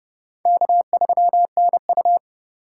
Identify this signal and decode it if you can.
Morse code, heard as a keyed tone, about 32 wpm, 700 Hz. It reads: K3NU
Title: USA Call Signs - 8x Character Spacing - 30wpm